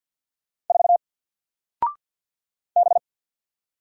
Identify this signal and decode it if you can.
Morse code, heard as a keyed tone, about 48 wpm, 700 Hz. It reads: 4 B
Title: Single Letter-Number Alt 50wpm